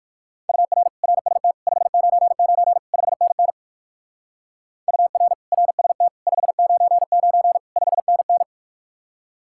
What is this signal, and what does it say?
Morse code, heard as a keyed tone, about 60 wpm, 700 Hz. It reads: UR RST 599 5NN UR RST 599 5NN